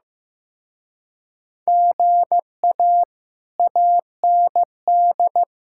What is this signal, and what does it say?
Morse code, heard as a keyed tone, about 16 wpm, 700 Hz. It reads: GA AND
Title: Speed-Racing Common QSO Phrases Encore - 15wpm